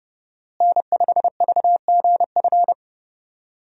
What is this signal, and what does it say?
Morse code, heard as a keyed tone, about 32 wpm, 700 Hz. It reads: N5VGF
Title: USA Call Signs - 5 Characters - Repeated 3x - 30wpm